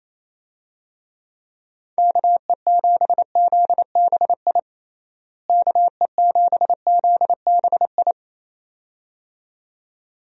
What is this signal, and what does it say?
Morse code, heard as a keyed tone, about 28 wpm, 700 Hz. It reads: KE7ZBI KE7ZBI
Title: USA Call Signs - 6 Characters - Repeated 2x - 28wpm